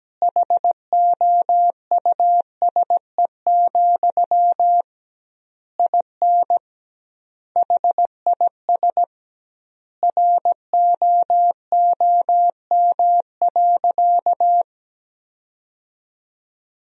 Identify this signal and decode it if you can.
Morse code, heard as a keyed tone, about 17 wpm, 700 Hz. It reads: HOUSE, IN HIS ROOM.
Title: 2x Word-Spacing - Sentences from Top 300 Words - 17wpm